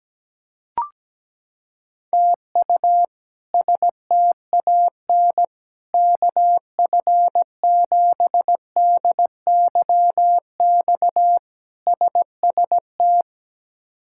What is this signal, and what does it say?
Morse code, heard as a keyed tone, about 17 wpm, 700 Hz. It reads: TU STAN KF7DYX SST
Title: K1USN SST QSO Practice - 17wpm